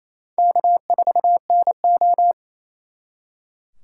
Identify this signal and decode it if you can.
Morse code, heard as a keyed tone, about 34 wpm, 700 Hz. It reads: K4NO
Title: Speed Racing - USA 1x2 Call Signs 28wpm